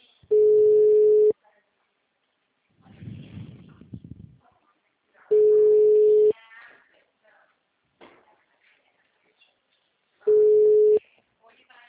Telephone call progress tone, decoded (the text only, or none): ringback tone